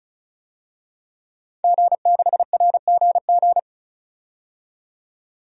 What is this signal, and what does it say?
Morse code, heard as a keyed tone, about 36 wpm, 700 Hz. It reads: G6RGG
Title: DXCC-100 Call Signs - 5 Characters - 35wpm